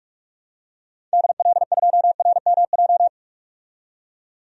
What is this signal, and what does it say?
Morse code, heard as a keyed tone, about 41 wpm, 700 Hz. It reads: DL2RMJ
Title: DXCC-100 Call Signs - 6 Characters - 45wpm